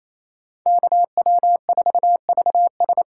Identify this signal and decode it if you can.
Morse code, heard as a keyed tone, about 36 wpm, 700 Hz. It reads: KW4VS